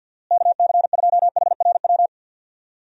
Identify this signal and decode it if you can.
Morse code, heard as a keyed tone, about 47 wpm, 700 Hz. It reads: KC2HRW
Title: USA Call Signs - 6 Characters - Repeated 3x - 50wpm